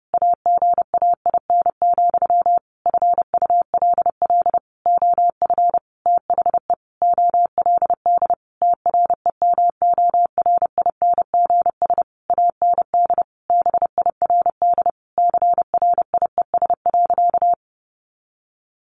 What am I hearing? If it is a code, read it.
Morse code, heard as a keyed tone, about 30 wpm, 700 Hz. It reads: AGAIN, FULL OF THE OLD TREMORINGS AND BIRD CRIES.
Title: A Sound Of Thunder 30wpm